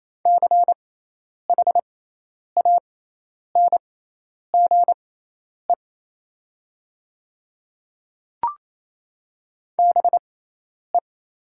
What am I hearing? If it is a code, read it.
Morse code, heard as a keyed tone, about 28 wpm, spaced as for 10 wpm, 700 Hz. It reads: CHANGE BE
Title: 6x Character Spacing - Top 200 Words - 28wpm